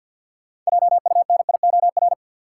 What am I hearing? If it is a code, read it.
Morse code, heard as a keyed tone, about 53 wpm, 700 Hz. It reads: JUNIOR
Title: Six Letter Words - 50wpm